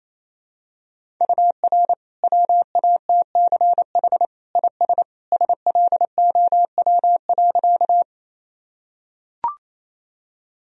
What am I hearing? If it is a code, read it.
Morse code, heard as a keyed tone, about 28 wpm, 700 Hz. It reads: UR WATCH IS SLOW.
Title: Sentences from Top 400 Words and Abbreviations 28wpm